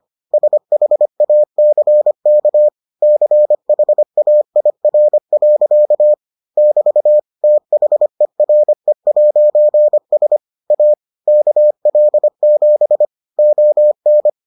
Morse code, heard as a keyed tone, about 25 wpm, 600 Hz. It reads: SHACK CHAIR. <BT> THERE'S A KL7 ON